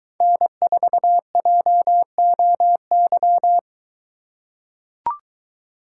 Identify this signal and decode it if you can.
Morse code, heard as a keyed tone, about 21 wpm, 700 Hz. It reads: N4JOY